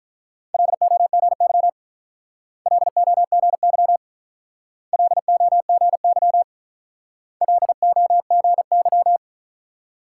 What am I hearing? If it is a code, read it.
Morse code, heard as a keyed tone, about 43 wpm, 700 Hz. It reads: LOGY LOGY LOGY LOGY
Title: Speed-Racing Multi-letter Phonemes - 35wpm